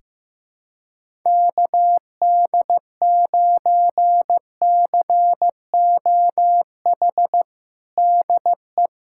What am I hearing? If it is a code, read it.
Morse code, heard as a keyed tone, about 15 wpm, 700 Hz. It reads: KD9COH DE
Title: USA Callsign DE Callsign - 6 Characters - 15wpm